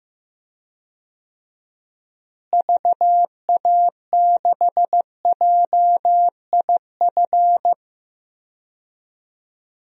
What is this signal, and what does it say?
Morse code, heard as a keyed tone, about 15 wpm, 700 Hz. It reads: VA6JIF
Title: DXCC-100 Call Signs - 6 Characters - Repeated 3x - 15wpm